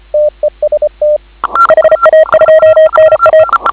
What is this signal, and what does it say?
Morse code, heard as a keyed tone, about 25 wpm, 590 Hz. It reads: TEST HA2NA